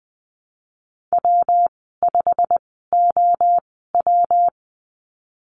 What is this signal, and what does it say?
Morse code, heard as a keyed tone, about 20 wpm, spaced as for 15 wpm, 700 Hz. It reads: W5OW
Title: 122 Lesson 20g - Number 3 - Call Signs 20wpm